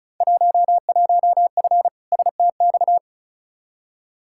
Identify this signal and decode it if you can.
Morse code, heard as a keyed tone, about 36 wpm, 700 Hz. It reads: 11F STX
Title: Field Day Exchanges Encore - 35wpm